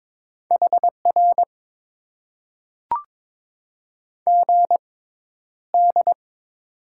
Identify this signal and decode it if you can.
Morse code, heard as a keyed tone, about 22 wpm, spaced as for 11 wpm, 700 Hz. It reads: HR GD
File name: Top 90 QSO Elements - 6x Character Spacing - 22wpm